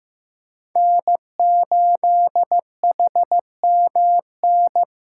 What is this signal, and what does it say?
Morse code, heard as a keyed tone, about 15 wpm, 700 Hz. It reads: N8HMN
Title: USA Call Signs - 5 Characters - 8x Character Spacing - 15wpm